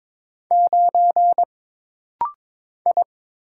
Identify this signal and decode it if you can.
Morse code, heard as a keyed tone, about 22 wpm, 700 Hz. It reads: 9 I
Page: This rapid-fire timing is geared to push your ICR (Instant Character Recognition) skills.